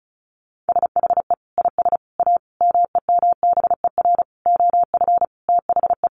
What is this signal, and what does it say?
Morse code, heard as a keyed tone, about 34 wpm, 700 Hz. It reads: SHE IS A MEMBER OF THE
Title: Sentences from Top 1100 Words 35wpm